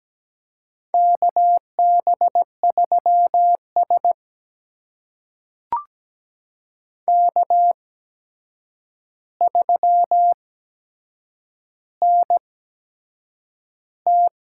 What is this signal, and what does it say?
Morse code, heard as a keyed tone, about 17 wpm, spaced as for 7 wpm, 700 Hz. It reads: KB3S K3NT
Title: USA Call Signs - 4 Characters - 8x Character Spacing - 17wpm